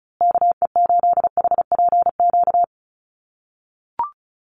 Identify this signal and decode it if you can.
Morse code, heard as a keyed tone, about 34 wpm, 700 Hz. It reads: KE8HPQ
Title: USA Call Signs Repeated 3x 35wpm